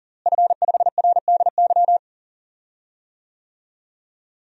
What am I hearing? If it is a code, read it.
Morse code, heard as a keyed tone, about 40 wpm, 700 Hz. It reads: F5RDY
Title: DXCC-100 Call Signs - 5 Characters - Repeated 3x - 40wpm